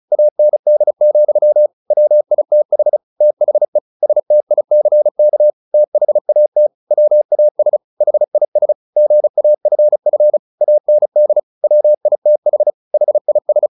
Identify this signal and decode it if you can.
Morse code, heard as a keyed tone, about 34 wpm, 600 Hz. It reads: AND, WITH THE STICK THAT WAS HIS GAFF AND WITH HIS
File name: the old man and the sea 2 35wpm